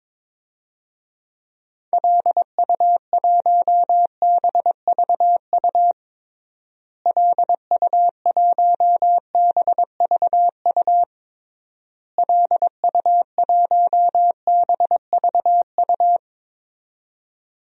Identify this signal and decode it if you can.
Morse code, heard as a keyed tone, about 22 wpm, 700 Hz. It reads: LU1BVU LU1BVU LU1BVU
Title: DXCC-100 Call Signs - 6 Characters - Repeated 3x - 22wpm